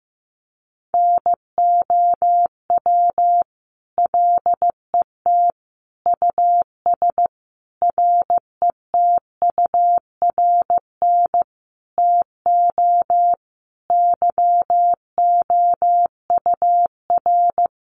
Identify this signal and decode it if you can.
Morse code, heard as a keyed tone, about 15 wpm, 700 Hz. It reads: NOW LET US RETURN TO YOUR
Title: Sentences from Top 800 Words 15wpm